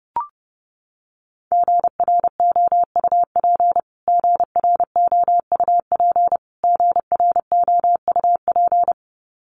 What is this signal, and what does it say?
Morse code, heard as a keyed tone, about 30 wpm, 700 Hz. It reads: GROUP GROUP GROUP
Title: Top 300 Words Repeated 30wpm